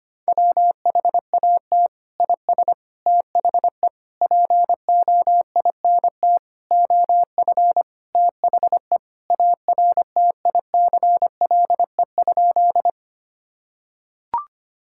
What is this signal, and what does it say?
Morse code, heard as a keyed tone, about 25 wpm, 700 Hz. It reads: WHAT IS THE POINT OF THE ARTICLE?
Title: Sentences from Top 1000 Words - Review Part 5 - 25wpm